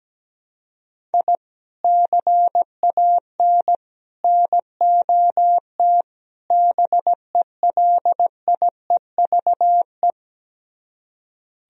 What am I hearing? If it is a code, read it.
Morse code, heard as a keyed tone, about 17 wpm, 700 Hz. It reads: I CAN NOT BELIEVE
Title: Speed-Racing - Sets of 4 Words 17wpm